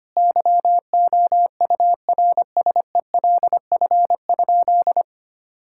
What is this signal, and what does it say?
Morse code, heard as a keyed tone, about 25 wpm, 700 Hz. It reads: YOURSELF?
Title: Sentences from Top 1600 Words 25wpm